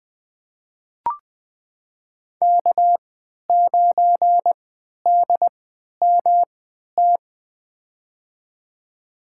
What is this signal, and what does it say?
Morse code, heard as a keyed tone, about 20 wpm, spaced as for 11 wpm, 700 Hz. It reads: K9DMT
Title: USA Call Signs - 5 Characters - 3x Character Spacing - 20wpm